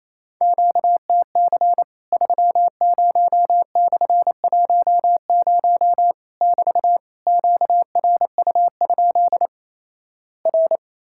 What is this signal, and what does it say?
Morse code, heard as a keyed tone, about 28 wpm, 700 Hz. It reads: QTC 30/10 <BT> QRU? R
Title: Worked All Europe DX Contest - QTC Exchange - 28wpm